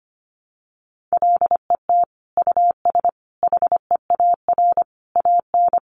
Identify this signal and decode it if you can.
Morse code, heard as a keyed tone, about 25 wpm, 700 Hz. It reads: LET US HEAR AN
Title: Sentences from Top 900 Words 25wpm